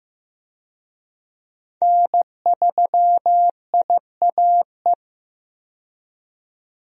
Morse code, heard as a keyed tone, about 15 wpm, 700 Hz. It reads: N3IAE
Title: USA Call Signs - 5 Characters - 15wpm